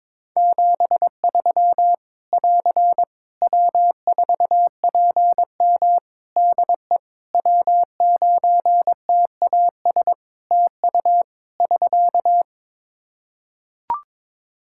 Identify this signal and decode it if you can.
Morse code, heard as a keyed tone, about 22 wpm, 700 Hz. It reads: 73 <AR> W4PM DE W9TAS TU <SK>
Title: K1USN SST QSO Practice 22wpm